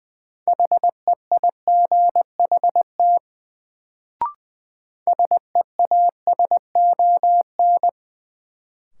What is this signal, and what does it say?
Morse code, heard as a keyed tone, about 20 wpm, 700 Hz. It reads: HEIGHT SEASON
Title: Sets of 5 Words - New Word Drill - 20wpm